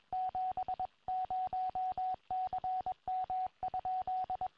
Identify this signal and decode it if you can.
Morse code, heard as a keyed tone, about 22 wpm, 740 Hz, with radio noise. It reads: 70CM?
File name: tropo / yl3iq